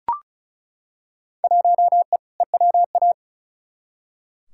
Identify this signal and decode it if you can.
Morse code, heard as a keyed tone, about 36 wpm, 700 Hz. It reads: 1E EWA
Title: Field Day Exchanges - 35wpm